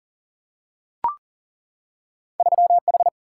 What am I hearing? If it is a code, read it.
Morse code, heard as a keyed tone, about 39 wpm, 700 Hz. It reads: 3H
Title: Single-Letter Single-Number 40wpm